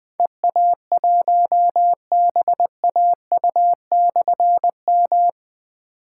Morse code, heard as a keyed tone, about 20 wpm, 700 Hz. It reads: EA1BAU/M
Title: Difficult Call Signs 20wpm